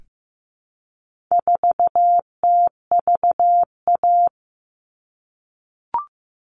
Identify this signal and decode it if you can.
Morse code, heard as a keyed tone, about 15 wpm, 700 Hz. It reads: 4TVA
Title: All Cut Numbers - 4-Digit 15wpm